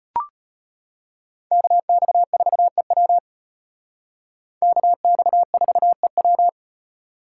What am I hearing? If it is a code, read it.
Morse code, heard as a keyed tone, about 33 wpm, 700 Hz. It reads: KX4EW KX4EW